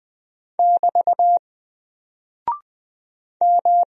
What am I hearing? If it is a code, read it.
Morse code, heard as a keyed tone, about 20 wpm, 700 Hz. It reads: <BT> M
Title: Single Character (Letter:Number:Punctuation:Prosign) Alt 20wpm